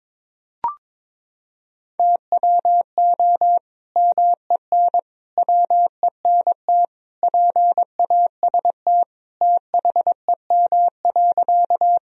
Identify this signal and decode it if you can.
Morse code, heard as a keyed tone, about 22 wpm, 700 Hz. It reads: TWO MEN WENT PAST THEM.
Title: Sentences from Top 1000 Words - Review Part 3 - 22wpm